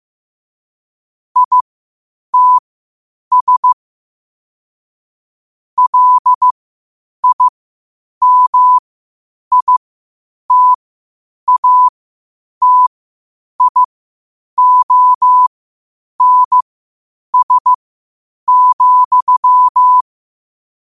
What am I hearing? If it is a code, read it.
Morse code, heard as a keyed tone, about 15 wpm, spaced as for 8 wpm, 1000 Hz. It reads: ITS LIMITATIONS,
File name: qotdfiles / 8wpm